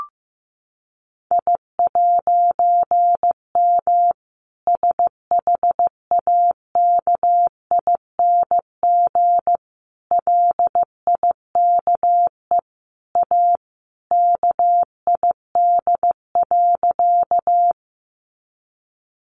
Morse code, heard as a keyed tone, about 15 wpm, 700 Hz. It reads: I'M SHAKING LIKE A KID.
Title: A Sound Of Thunder 15wpm